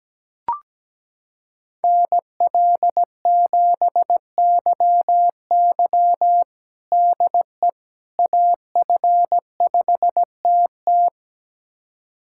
Morse code, heard as a keyed tone, about 17 wpm, 700 Hz. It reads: NL7YY DE AF5TT
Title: USA Callsign DE Callsign - 5 Characters - 17wpm